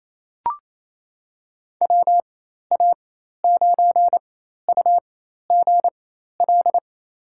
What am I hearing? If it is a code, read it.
Morse code, heard as a keyed tone, about 28 wpm, spaced as for 13 wpm, 700 Hz. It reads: WA9UGL
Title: USA Call Signs - 6 Characters - 4x Character Spacing - 28wpm